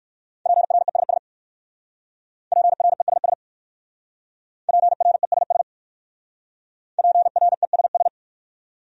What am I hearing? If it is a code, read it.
Morse code, heard as a keyed tone, about 58 wpm, 700 Hz. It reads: PRESS PRESS PRESS PRESS